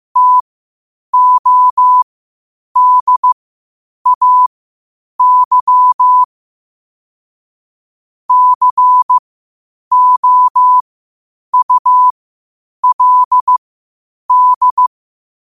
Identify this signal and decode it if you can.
Morse code, heard as a keyed tone, about 15 wpm, spaced as for 8 wpm, 1000 Hz. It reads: TODAY COULD
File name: qotdfiles / 8wpm